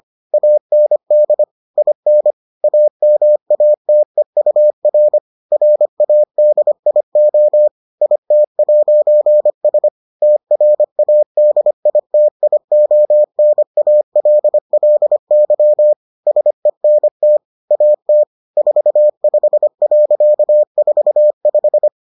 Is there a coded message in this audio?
Yes: Morse code from 0.3 to 21.9 s, heard as a keyed tone, about 25 wpm, 600 Hz. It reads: AND IN AMATEUR RADIO IT'S TRADITIONALLY SENT AT 45.45